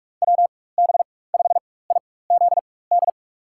Morse code, heard as a keyed tone, about 47 wpm, spaced as for 21 wpm, 700 Hz. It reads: WB5IZD
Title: USA Call Signs - 4x Character Spacing - 45wpm